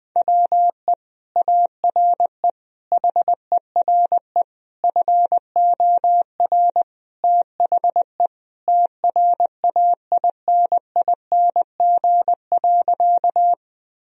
Morse code, heard as a keyed tone, about 20 wpm, 700 Hz. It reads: WE ARE HERE FOR THE TRAINING.
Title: Sentences from Top 1000 Words - Review Part 4 - 20wpm